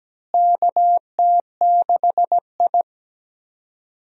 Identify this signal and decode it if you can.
Morse code, heard as a keyed tone, about 17 wpm, 700 Hz. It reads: KT6I
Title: USA Call Signs - 4 Characters - Repeated 3x - 17wpm